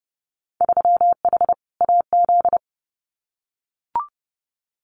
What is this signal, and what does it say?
Morse code, heard as a keyed tone, about 30 wpm, 700 Hz. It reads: 3H AZ
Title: Winter Field Day Exchanges Encore 30wpm